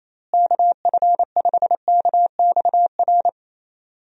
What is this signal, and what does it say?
Morse code, heard as a keyed tone, about 28 wpm, 700 Hz. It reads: KF5KXR
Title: USA Call Signs - 6 Characters - Repeated 3x - 28wpm